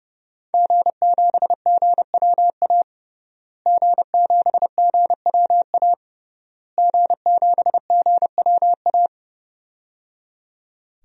Morse code, heard as a keyed tone, about 30 wpm, 700 Hz. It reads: G7GWA G7GWA G7GWA
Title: DXCC-100 Call Signs - 5 Characters - Repeated 3x - 30wpm